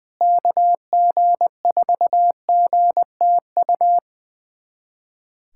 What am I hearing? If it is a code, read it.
Morse code, heard as a keyed tone, about 20 wpm, 700 Hz. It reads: KG4GTU
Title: USA Callsign DE Callsign - 6 Characters - 20wpm